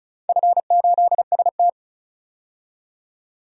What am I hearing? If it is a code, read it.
Morse code, heard as a keyed tone, about 36 wpm, 700 Hz. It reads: F8ST
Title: DXCC-100 Call Signs - 4 Characters - 35wpm